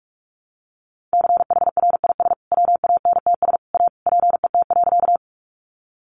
Morse code, heard as a keyed tone, about 44 wpm, 700 Hz. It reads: CHRIS WANTS A PET.
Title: Easy Phrases 45wpm